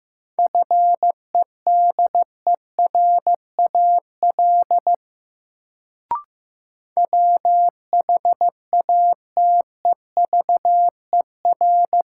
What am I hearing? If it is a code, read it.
Morse code, heard as a keyed tone, about 16 wpm, 700 Hz. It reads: FEDERAL WHATEVER
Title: Sentences from Top 1100 Words - New Word Drill Alt - 15wpm